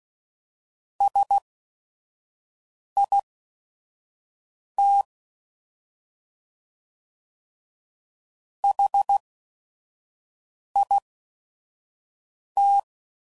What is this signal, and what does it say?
Morse code, heard as a keyed tone, about 16 wpm, spaced as for 5 wpm, 800 Hz. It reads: SIT HIT
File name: Prefixes / By First Letter / S vs Hs